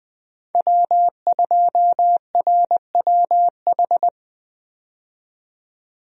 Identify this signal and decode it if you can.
Morse code, heard as a keyed tone, about 20 wpm, 700 Hz. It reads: W2RWH
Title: USA Call Signs - 5 Characters - 20wpm